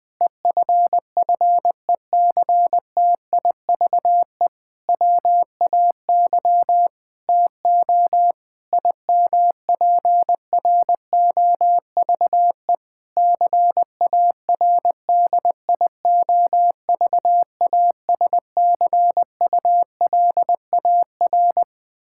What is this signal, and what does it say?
Morse code, heard as a keyed tone, about 20 wpm, 700 Hz. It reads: EFFECTIVE WAY TO IMPROVE CARDIOVASCULAR
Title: Difficult Sentences with Hints - 20wpm